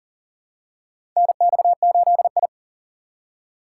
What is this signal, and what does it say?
Morse code, heard as a keyed tone, about 40 wpm, 700 Hz. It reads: NX8I
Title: USA Call Signs - 4 Characters - 6x Character Spacing - 40wpm